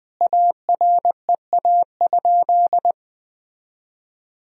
Morse code, heard as a keyed tone, about 20 wpm, 700 Hz. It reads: AREA?
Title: Sentences from Top 1000 Words - Review Part 2 - 20wpm